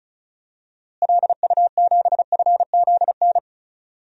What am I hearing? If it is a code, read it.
Morse code, heard as a keyed tone, about 34 wpm, 700 Hz. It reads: LU7FZN
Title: DXCC-100 Call Signs - 6 Characters - Repeated 2x - 35wpm